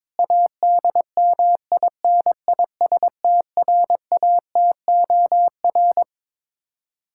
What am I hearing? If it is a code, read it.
Morse code, heard as a keyed tone, about 22 wpm, 700 Hz. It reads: ADMINISTRATOR
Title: K1USN SST QSO Practice 22wpm